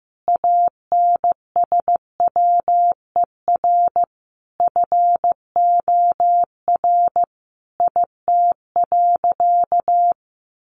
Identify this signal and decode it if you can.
Morse code, heard as a keyed tone, about 15 wpm, 700 Hz. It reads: ANSWER FOR IT.
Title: Sentences from Top 300 Words 15wpm